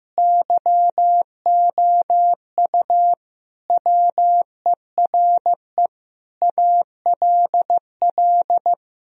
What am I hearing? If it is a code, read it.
Morse code, heard as a keyed tone, about 15 wpm, 700 Hz. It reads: YOU WERE ALL
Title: Sentences from Top 1000 Words - Review Part 1 - 15wpm